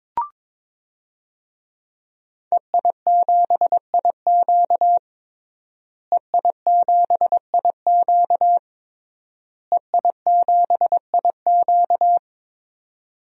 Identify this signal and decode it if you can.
Morse code, heard as a keyed tone, about 22 wpm, 700 Hz. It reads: EI7IQ EI7IQ EI7IQ
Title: DXCC-100 Call Signs - 5 Characters - Repeated 3x - 22wpm